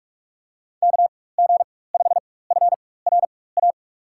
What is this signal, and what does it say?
Morse code, heard as a keyed tone, about 42 wpm, spaced as for 21 wpm, 700 Hz. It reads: KG5FRA
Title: USA Call Signs - 6 Characters - 4x Character Spacing - 45wpm